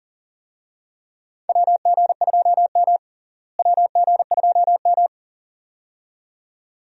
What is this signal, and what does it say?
Morse code, heard as a keyed tone, about 40 wpm, 700 Hz. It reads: WG2M WG2M
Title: USA Call Signs - 4 Characters - Repeated 2x - 40wpm